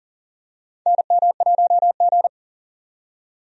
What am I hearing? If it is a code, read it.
Morse code, heard as a keyed tone, about 40 wpm, 700 Hz. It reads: NM1G
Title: USA Call Signs - 4 Characters - 40wpm